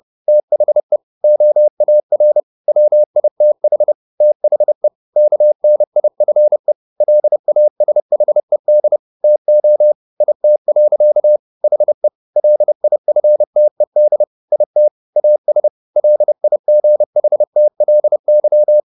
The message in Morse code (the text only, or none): THE OAR WITH THE KNIFE LASHED TO IT. HE LIFTED IT AS LIGHTLY